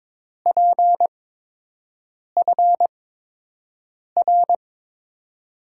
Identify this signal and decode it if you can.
Morse code, heard as a keyed tone, about 22 wpm, 700 Hz. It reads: P F R
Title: USA Call Signs - 5 Characters - 8x Character Spacing - 22wpm